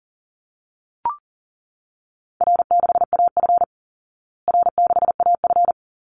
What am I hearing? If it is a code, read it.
Morse code, heard as a keyed tone, about 40 wpm, 700 Hz. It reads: R6AF R6AF
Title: Easy Call Signs Repeated 2x - 40wpm